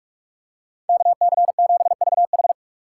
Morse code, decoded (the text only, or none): KC7VH